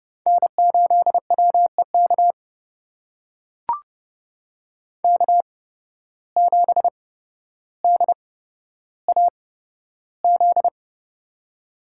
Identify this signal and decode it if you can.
Morse code, heard as a keyed tone, about 30 wpm, spaced as for 13 wpm, 700 Hz. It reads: N8WEK K7DAZ
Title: USA Call Signs - 5 Characters - 8x Character Spacing - 30wpm